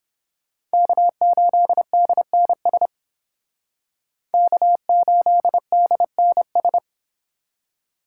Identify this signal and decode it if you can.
Morse code, heard as a keyed tone, about 29 wpm, 700 Hz. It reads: K8DNS K8DNS